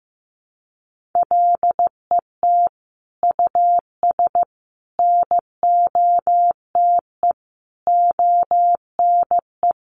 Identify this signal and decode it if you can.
Morse code, heard as a keyed tone, about 15 wpm, 700 Hz. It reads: LET US NOTE ONE
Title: Sentences from Top 1500 Words 15wpm